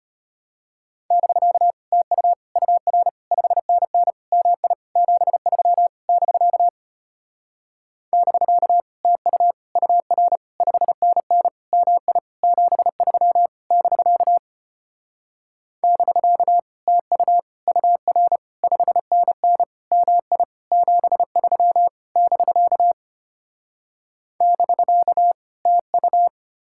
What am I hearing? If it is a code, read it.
Morse code, heard as a keyed tone, about 33 wpm, 700 Hz. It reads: <BK> TU UR 5NN MI 73 <BK> <BK> TU UR 5NN MI 73 <BK> <BK> TU UR 5NN MI 73 <BK> <BK> TU